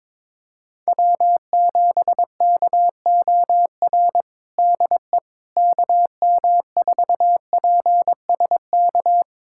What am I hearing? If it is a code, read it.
Morse code, heard as a keyed tone, about 22 wpm, 700 Hz. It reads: W7KOR DE KM4PSK
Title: USA Callsign DE Callsign 22wpm